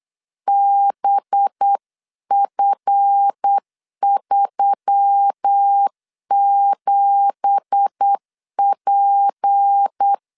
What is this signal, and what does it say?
Morse code, heard as a keyed tone, about 8 wpm, 800 Hz. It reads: BF37P
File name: CW / DET / 1 9 1500C S B 8wpm 5624